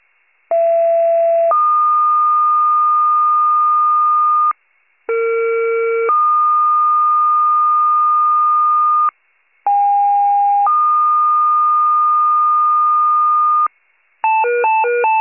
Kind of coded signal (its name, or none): two-tone page